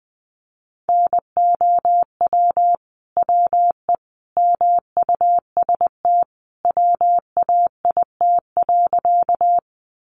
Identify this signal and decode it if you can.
Morse code, heard as a keyed tone, about 20 wpm, 700 Hz. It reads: NOW WE MUST WAIT.
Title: Sentences from Top 500 Words 20wpm